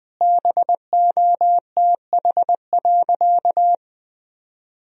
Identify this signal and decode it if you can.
Morse code, heard as a keyed tone, about 20 wpm, 700 Hz. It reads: BOTH.
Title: Sentences from Top 300 Words and Abbreviations 20wpm